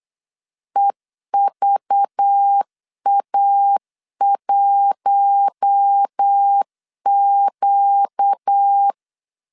Morse code, heard as a keyed tone, about 8 wpm, 800 Hz. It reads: EVA1Q
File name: LEZIONI / 6 PKX0 / 6 P 8wpm 3025